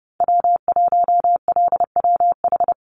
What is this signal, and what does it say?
Morse code, heard as a keyed tone, about 30 wpm, 700 Hz. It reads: W1LWH
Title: Call Signs Repeated 3x - 30wpm